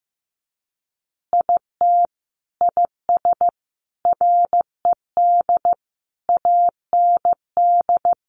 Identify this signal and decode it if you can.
Morse code, heard as a keyed tone, about 15 wpm, 700 Hz. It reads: IT IS RED AND
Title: Additional Easy Phrases 15wpm